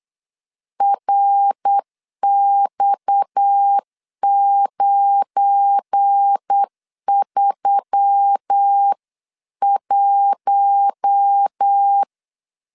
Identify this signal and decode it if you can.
Morse code, heard as a keyed tone, about 8 wpm, 800 Hz. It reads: RX931
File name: LEZIONI / 7 GF9Z / 7 GF9 8wpm 3545